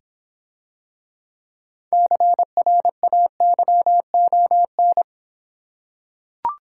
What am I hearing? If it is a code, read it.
Morse code, heard as a keyed tone, about 26 wpm, 700 Hz. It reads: CRAYON